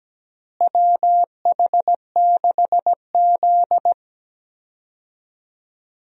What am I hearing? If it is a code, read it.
Morse code, heard as a keyed tone, about 17 wpm, 700 Hz. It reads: WH6Z
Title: USA Call Signs - 4 Characters - Repeated 2x - 17wpm